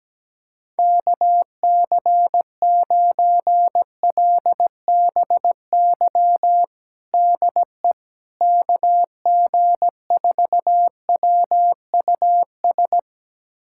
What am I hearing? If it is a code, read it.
Morse code, heard as a keyed tone, about 17 wpm, 700 Hz. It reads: KC9LBY DE KG4WUS
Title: USA Callsign DE Callsign - 6 Characters - 17wpm